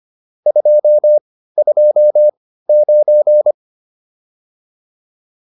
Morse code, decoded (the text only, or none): 229